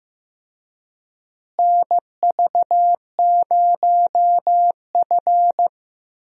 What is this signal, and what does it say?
Morse code, heard as a keyed tone, about 16 wpm, 700 Hz. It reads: NV0F
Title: USA Call Signs - 4 Characters - Repeated 3x - 15wpm